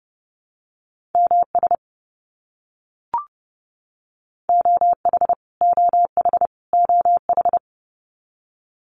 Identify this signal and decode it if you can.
Morse code, heard as a keyed tone, about 30 wpm, 700 Hz. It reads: MS OH OH OH
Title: US States Repeated 3x 30wpm